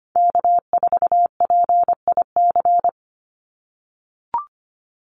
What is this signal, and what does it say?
Morse code, heard as a keyed tone, about 25 wpm, 700 Hz. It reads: K4PIC
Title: NA SOTA Chaser Callsigns 25wpm